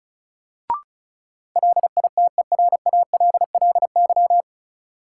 Morse code, heard as a keyed tone, about 36 wpm, 700 Hz. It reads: LITERALLY
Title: Sentences from Top 1400 Words - New Word Drill - 35wpm